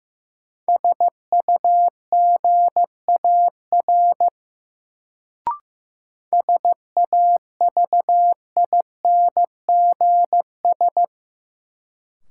Morse code, heard as a keyed tone, about 15 wpm, 700 Hz. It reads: SUGAR SAVINGS
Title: Sentences from Top 1200 Words - New Word Drill - 15wpm